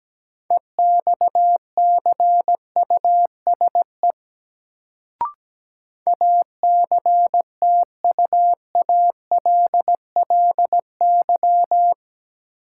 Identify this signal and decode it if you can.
Morse code, heard as a keyed tone, about 17 wpm, 700 Hz. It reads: EXCUSE ACTUALLY
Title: Sentences from Top 700 Words - New Word Drill - 17wpm